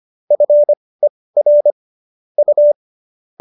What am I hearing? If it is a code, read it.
Morse code, heard as a keyed tone, about 25 wpm, spaced as for 18 wpm, 590 Hz. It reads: FER U